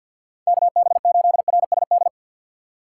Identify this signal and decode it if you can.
Morse code, heard as a keyed tone, about 53 wpm, 700 Hz. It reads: KB8RSD
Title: USA Call Signs - 6 Characters - 4x Character Spacing - 50wpm